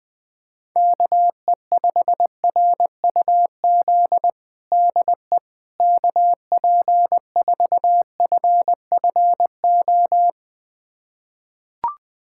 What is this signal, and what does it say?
Morse code, heard as a keyed tone, about 20 wpm, 700 Hz. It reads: KE5RUZ DE KP4FFO
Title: USA Callsign DE Callsign - 6 Characters - 20wpm